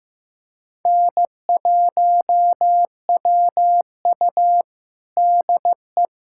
Morse code, heard as a keyed tone, about 15 wpm, 700 Hz. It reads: N1WU DE
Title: USA Callsign DE Callsign - 4 Characters - 15wpm